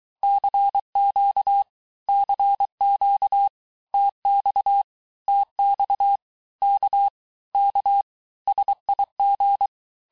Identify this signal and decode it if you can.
Morse code, heard as a keyed tone, about 23 wpm, 800 Hz. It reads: CQ CQ TX TX K K SIG
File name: cours-du-lundi files / cours-1588-texte-abreviations-vitesse-17mpm-puis-22mpm